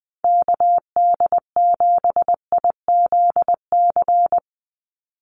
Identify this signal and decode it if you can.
Morse code, heard as a keyed tone, about 20 wpm, 700 Hz. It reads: KD7IZC
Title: USA Call Signs 20wpm